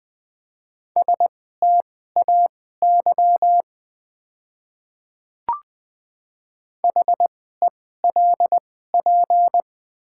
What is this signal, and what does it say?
Morse code, heard as a keyed tone, about 21 wpm, spaced as for 15 wpm, 700 Hz. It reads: STAY HELP
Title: 2x Character Spacing - Top 500 Words - 20wpm